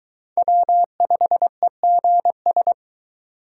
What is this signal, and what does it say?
Morse code, heard as a keyed tone, about 22 wpm, 700 Hz. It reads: W5EGS